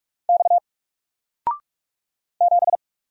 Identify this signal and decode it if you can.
Morse code, heard as a keyed tone, about 45 wpm, 700 Hz. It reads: X 7
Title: Single Letter-Number 45wpm